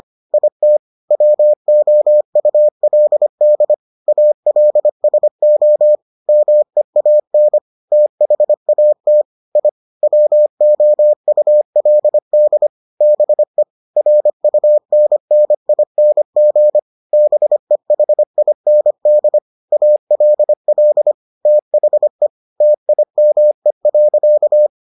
Morse code, heard as a keyed tone, about 25 wpm, 600 Hz. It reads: IT WOULD ALSO MEAN THAT I WOULD BE RUNNING BEHIND ALL THE TIME.